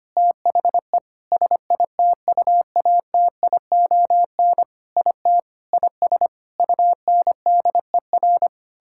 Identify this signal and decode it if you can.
Morse code, heard as a keyed tone, about 25 wpm, 700 Hz. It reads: THE SITUATION IT IS UNDER
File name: practice / 25wpm / Sentences with 9-Word Phrases - 25wpm